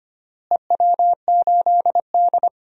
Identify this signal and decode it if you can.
Morse code, heard as a keyed tone, about 25 wpm, 700 Hz. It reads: EW8D
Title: DXCC-100 Call Signs - 4 Characters - Repeated 2x - 25wpm